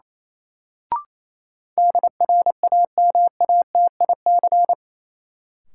Morse code, heard as a keyed tone, about 28 wpm, 700 Hz. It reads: DRAMATIC
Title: Sentences from Top 1300 Words - New Word Drill - 28wpm